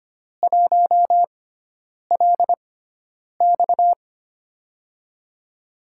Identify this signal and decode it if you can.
Morse code, heard as a keyed tone, about 25 wpm, spaced as for 9 wpm, 700 Hz. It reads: 1LX
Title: USA Call Signs - 5 Characters - 6x Character Spacing - 25wpm